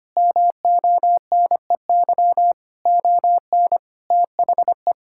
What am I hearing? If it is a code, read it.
Morse code, heard as a keyed tone, about 25 wpm, 700 Hz. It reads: MONEY ON THE
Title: Sentences from Top 1000 Words - Review Part 6 - 25wpm